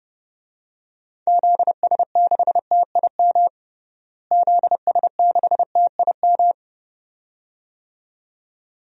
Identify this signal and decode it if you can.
Morse code, heard as a keyed tone, about 30 wpm, 700 Hz. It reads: ZS6TIM ZS6TIM
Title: DXCC-100 Call Signs - 6 Characters - Repeated 2x - 30wpm